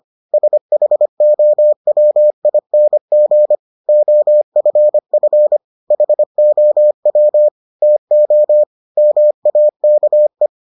Morse code, heard as a keyed tone, about 25 wpm, 600 Hz. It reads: SHOWING OFF HOW TO MAKE